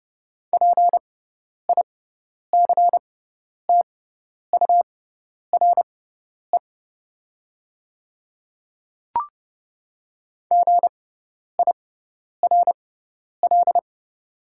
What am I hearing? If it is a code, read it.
Morse code, heard as a keyed tone, about 30 wpm, spaced as for 10 wpm, 700 Hz. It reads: PICTURE GIRL
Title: 6x Character Spacing - Top 500 Words - 30wpm